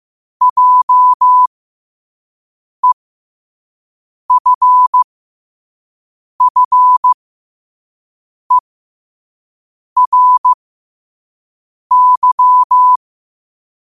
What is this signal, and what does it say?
Morse code, heard as a keyed tone, about 15 wpm, spaced as for 5 wpm, 1000 Hz. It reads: JEFFERY
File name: qotdfiles / 5wpm